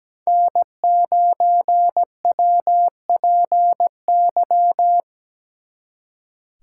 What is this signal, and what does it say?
Morse code, heard as a keyed tone, about 20 wpm, 700 Hz. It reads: N9WPY
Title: Speed Racing - USA Call Signs 17wpm